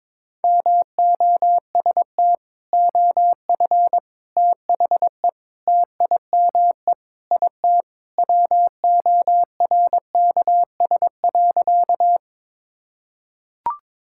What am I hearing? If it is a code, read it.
Morse code, heard as a keyed tone, about 22 wpm, 700 Hz. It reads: MOST OF THE TIME IT WORKS.
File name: practice / 22wpm / Sentences with 3-Word Phrases - 22wpm